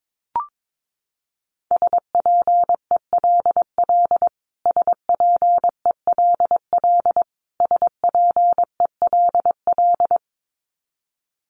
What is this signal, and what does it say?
Morse code, heard as a keyed tone, about 22 wpm, 700 Hz. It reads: SPELL SPELL SPELL
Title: Top 200 Words Repeated 22wpm